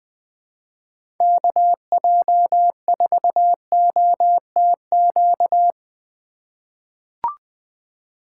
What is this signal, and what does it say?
Morse code, heard as a keyed tone, about 20 wpm, 700 Hz. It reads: KJ4OTQ
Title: USA Call Signs - 6 Characters - 2x Character Spacing - 20wpm